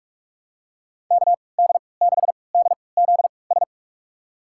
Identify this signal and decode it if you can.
Morse code, heard as a keyed tone, about 46 wpm, spaced as for 26 wpm, 700 Hz. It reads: KD6DZS
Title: USA Call Signs - 6 Characters - 3x Character Spacing - 45wpm